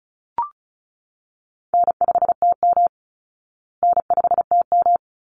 Morse code, heard as a keyed tone, about 34 wpm, 700 Hz. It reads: N5TM N5TM
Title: Call Signs Repeated 3x - 35wpm